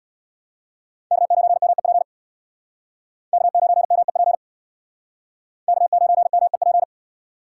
Morse code, heard as a keyed tone, about 65 wpm, 700 Hz. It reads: K9MEP K9MEP K9MEP